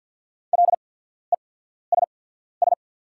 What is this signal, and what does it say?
Morse code, heard as a keyed tone, about 48 wpm, spaced as for 14 wpm, 700 Hz. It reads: LESS
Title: 8x Character Spacing - Top 500 Words - 50wpm